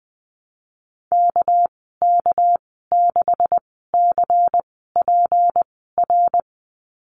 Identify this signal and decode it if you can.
Morse code, heard as a keyed tone, about 20 wpm, spaced as for 14 wpm, 700 Hz. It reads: KK6CPR
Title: 219 Lesson 35g - Number 0 - Call Signs 20wpm